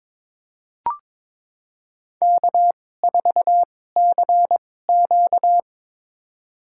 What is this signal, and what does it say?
Morse code, heard as a keyed tone, about 21 wpm, spaced as for 16 wpm, 700 Hz. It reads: K4CQ
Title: USA Call Signs - 4 Characters - 2x Character Spacing - 22wpm